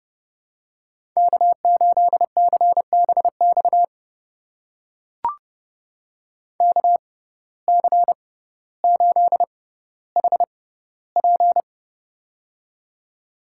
Morse code, heard as a keyed tone, about 30 wpm, spaced as for 15 wpm, 700 Hz. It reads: K8CBX KC8HP
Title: USA Call Signs - 5 Characters - 6x Character Spacing - 30wpm